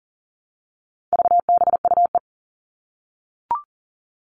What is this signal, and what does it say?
Morse code, heard as a keyed tone, about 40 wpm, 700 Hz. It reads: VBUE
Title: All Cut Numbers - 4-Digit 40wpm